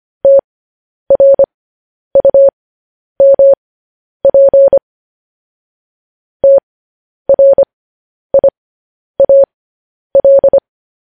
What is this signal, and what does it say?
Morse code, heard as a keyed tone, about 25 wpm, spaced as for 10 wpm, 550 Hz. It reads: TRUMP TRIAL
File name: podcast / News headlines-1x-25x10